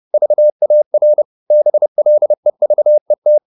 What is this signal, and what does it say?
Morse code, heard as a keyed tone, about 30 wpm, 600 Hz. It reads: VAR BLEVET
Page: Ferie 30 wpm